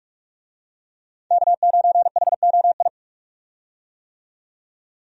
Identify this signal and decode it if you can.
Morse code, heard as a keyed tone, about 45 wpm, 700 Hz. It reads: K9HOI
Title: USA Call Signs - 5 Characters - 45wpm